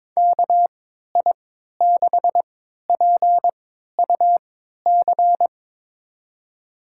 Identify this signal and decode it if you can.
Morse code, heard as a keyed tone, about 22 wpm, spaced as for 13 wpm, 700 Hz. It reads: KI6PUC
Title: USA Call Signs - 3x Character Spacing - 22wpm